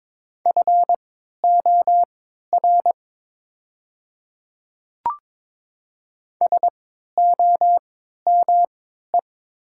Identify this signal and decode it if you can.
Morse code, heard as a keyed tone, about 22 wpm, spaced as for 12 wpm, 700 Hz. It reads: FOR SOME
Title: 3x Character Spacing - Top 100 Words - 22wpm